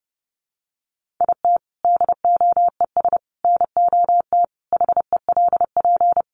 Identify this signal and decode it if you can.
Morse code, heard as a keyed tone, about 30 wpm, 700 Hz. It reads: IT DOES NOT HELP
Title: Sentences from Top 1500 Words 30wpm